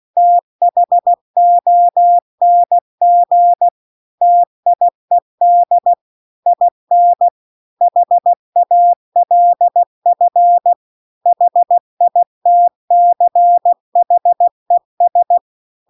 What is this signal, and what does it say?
Morse code, heard as a keyed tone, about 16 wpm, 700 Hz. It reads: THONG TIED IN HALF HITCHES
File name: CW-The-Big-Twoheaded-River-2-16WPM